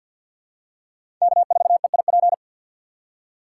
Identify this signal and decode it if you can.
Morse code, heard as a keyed tone, about 51 wpm, 700 Hz. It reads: K4EIP
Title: USA Call Signs - 5 Characters - 4x Character Spacing - 50wpm